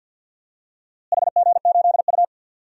Morse code, heard as a keyed tone, about 53 wpm, 700 Hz. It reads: HG8U
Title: DXCC-100 Call Signs - 4 Characters - Repeated 3x - 50wpm